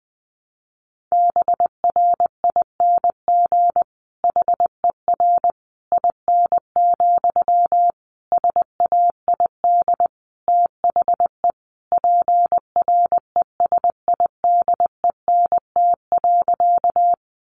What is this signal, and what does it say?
Morse code, heard as a keyed tone, about 20 wpm, 700 Hz. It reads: BRING HER IN, SAID THE PRESIDENT.
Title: Sentences from Top 1000 Words 20wpm